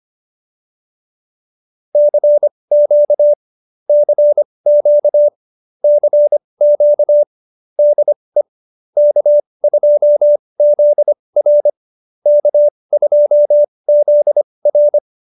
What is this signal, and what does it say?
Morse code, heard as a keyed tone, about 25 wpm, spaced as for 20 wpm, 590 Hz. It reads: CQ CQ CQ DE K2ZR K2ZR
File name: QSO302 20